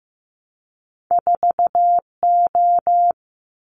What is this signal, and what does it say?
Morse code, heard as a keyed tone, about 15 wpm, 700 Hz. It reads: 4O
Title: Winter Field Day Exchanges Encore 15wpm